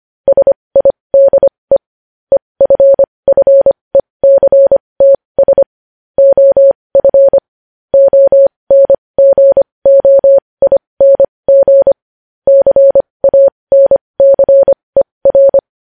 Morse code, heard as a keyed tone, about 25 wpm, spaced as for 20 wpm, 550 Hz. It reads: SIDE EFFECTS OF ONGOING CANCER
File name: podcast / News headlines-1x-25x20